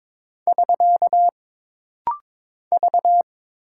Morse code, heard as a keyed tone, about 22 wpm, 700 Hz. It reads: <SK> V
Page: This rapid-fire timing is geared to push your ICR (Instant Character Recognition) skills.